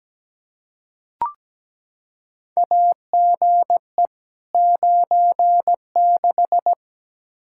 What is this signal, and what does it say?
Morse code, heard as a keyed tone, about 17 wpm, 700 Hz. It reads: AGE 96
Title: Practice QSO - Exchange 3 - More Info - 17wpm